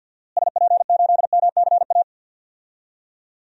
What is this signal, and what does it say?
Morse code, heard as a keyed tone, about 46 wpm, 700 Hz. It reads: SP8MCA
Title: DXCC-100 Call Signs - 6 Characters - Repeated 3x - 50wpm